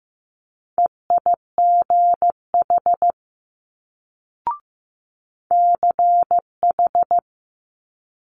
Morse code, heard as a keyed tone, about 15 wpm, 700 Hz. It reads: EIGH CH
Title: Multi-letter Phonemes Quick Fire 15wpm